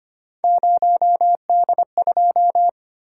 Morse code, heard as a keyed tone, about 25 wpm, 700 Hz. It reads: 0D2
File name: Three Letter-Number Combinations - 25wpm